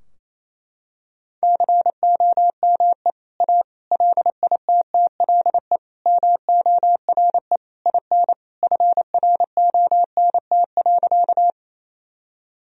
Morse code, heard as a keyed tone, about 27 wpm, 700 Hz. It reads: COME A LITTLE MORE IN FRONT.
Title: Sentences from Top 1000 Words - Review Part 2 - 28wpm